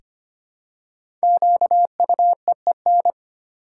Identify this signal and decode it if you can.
Morse code, heard as a keyed tone, about 25 wpm, 700 Hz. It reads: QUEEN
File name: 5 Letter Words - First 3 Letters Given - 25wpm